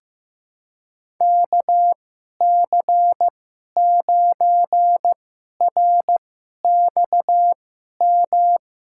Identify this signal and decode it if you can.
Morse code, heard as a keyed tone, about 15 wpm, spaced as for 11 wpm, 700 Hz. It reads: KC9RXM
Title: USA Call Signs - 6 Characters - 2x Character Spacing - 15wpm